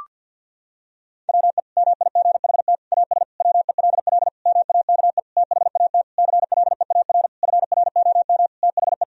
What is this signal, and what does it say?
Morse code, heard as a keyed tone, about 51 wpm, 700 Hz. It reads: WE MIGHT AS WELL MAKE THAT CLEAR FROM THE
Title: Sentences from Top 1000 Words - Review Part 1 - 50wpm